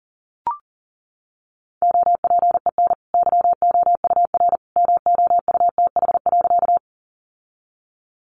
Morse code, heard as a keyed tone, about 40 wpm, 700 Hz. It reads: OPEN YOUR MOUTH.
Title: Sentences from Top 1500 Words 40wpm